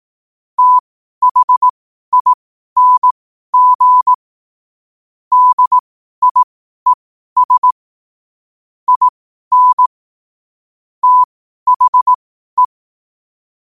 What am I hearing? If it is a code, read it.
Morse code, heard as a keyed tone, about 18 wpm, spaced as for 12 wpm, 1000 Hz. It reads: THING DIES IN THE
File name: qotdfiles / 12wpm